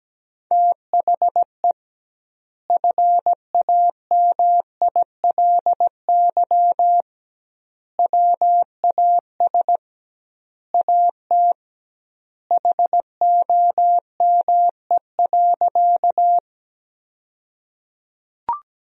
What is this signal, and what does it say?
Morse code, heard as a keyed tone, about 17 wpm, 700 Hz. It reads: THE FAMILY WAS AT HOME.
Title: 2x Word-Spacing - Sentences from Top 400 Words - 17wpm